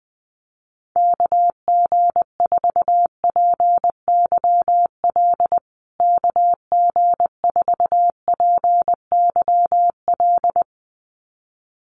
Morse code, heard as a keyed tone, about 20 wpm, 700 Hz. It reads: KG4PYL KG4PYL
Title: USA Call Signs Repeated 2x 20wpm